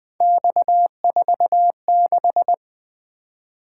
Morse code, heard as a keyed tone, about 20 wpm, 700 Hz. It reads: X46
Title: Three Letter-Number Combinations - 20wpm